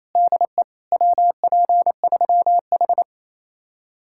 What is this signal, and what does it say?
Morse code, heard as a keyed tone, about 35 wpm, 700 Hz. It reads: DE WP3H